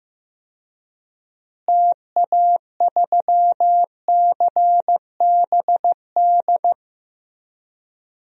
Morse code, heard as a keyed tone, about 15 wpm, 700 Hz. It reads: TA3CBD
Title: DXCC-100 Call Signs - 6 Characters - Repeated 3x - 15wpm